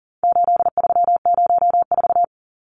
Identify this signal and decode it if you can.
Morse code, heard as a keyed tone, about 40 wpm, 700 Hz. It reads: 8304
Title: Numbers - 4-Digit 40wpm